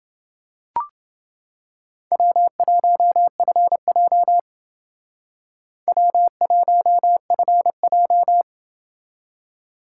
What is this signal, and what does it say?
Morse code, heard as a keyed tone, about 28 wpm, 700 Hz. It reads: W1FJ W1FJ